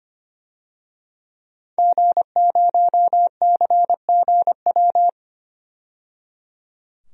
Morse code, heard as a keyed tone, about 25 wpm, 700 Hz. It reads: G0CGW
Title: DXCC-100 Call Signs - 5 Characters - 25wpm